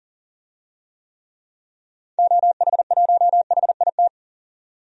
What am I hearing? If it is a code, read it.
Morse code, heard as a keyed tone, about 40 wpm, 700 Hz. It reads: OH1HIT
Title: DXCC-100 Call Signs - 6 Characters - 40wpm